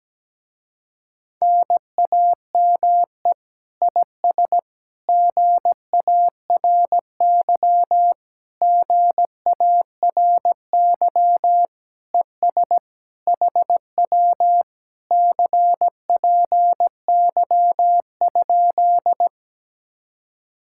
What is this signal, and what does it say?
Morse code, heard as a keyed tone, about 17 wpm, 700 Hz. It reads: NAME IS GARY GARY ES HW CPY?
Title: Speed-Racing Common QSO Phrases - 17wpm